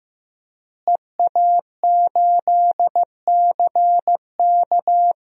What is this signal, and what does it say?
Morse code, heard as a keyed tone, about 15 wpm, 700 Hz. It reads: EA8CK
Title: DXCC-100 Call Signs - 5 Characters - Repeated 2x - 15wpm